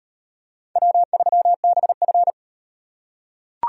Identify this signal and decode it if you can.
Morse code, heard as a keyed tone, about 36 wpm, 700 Hz. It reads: W3BF